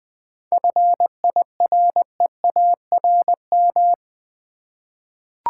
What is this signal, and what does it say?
Morse code, heard as a keyed tone, about 21 wpm, 700 Hz. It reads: FIREARM
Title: 7 Letter Words - First 3 Letters Given - 20wpm